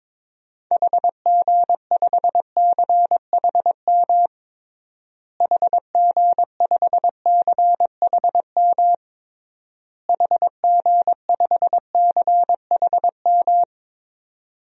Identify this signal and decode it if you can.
Morse code, heard as a keyed tone, about 22 wpm, 700 Hz. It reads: HG5CHM HG5CHM HG5CHM
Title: DXCC-100 Call Signs - 6 Characters - Repeated 3x - 22wpm